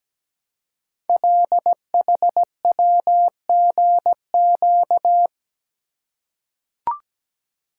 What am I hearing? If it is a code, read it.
Morse code, heard as a keyed tone, about 17 wpm, 700 Hz. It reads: LHWGQ
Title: Code Groups - 5 Letters - 17wpm